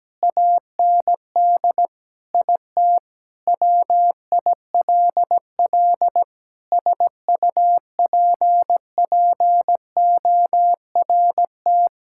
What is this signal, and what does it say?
Morse code, heard as a keyed tone, about 17 wpm, 700 Hz. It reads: AND IT WILL SUPPORT
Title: Sentences from Top 1000 Words - Review Part 5 - 17wpm